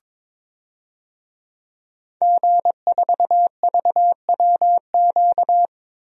Morse code, heard as a keyed tone, about 22 wpm, 700 Hz. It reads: G4VWQ
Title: DXCC-100 Call Signs - 5 Characters - Repeated 2x - 22wpm